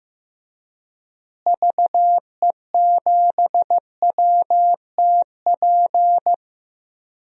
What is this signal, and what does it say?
Morse code, heard as a keyed tone, about 15 wpm, 700 Hz. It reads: VE7WTP
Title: DXCC-100 Call Signs - 6 Characters - Repeated 3x - 15wpm